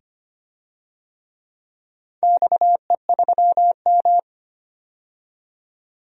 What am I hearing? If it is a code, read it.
Morse code, heard as a keyed tone, about 25 wpm, 700 Hz. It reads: XE3M
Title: DXCC-100 Call Signs - 4 Characters - 25wpm